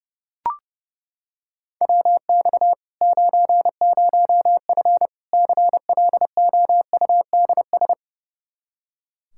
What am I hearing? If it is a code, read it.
Morse code, heard as a keyed tone, about 32 wpm, 700 Hz. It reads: WX 90F CLOUDS
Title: Practice QSO - Exchange 2 - Info - 30wpm